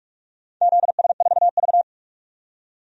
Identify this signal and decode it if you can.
Morse code, heard as a keyed tone, about 46 wpm, 700 Hz. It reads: ZS4V
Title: DXCC-100 Call Signs - 4 Characters - 45wpm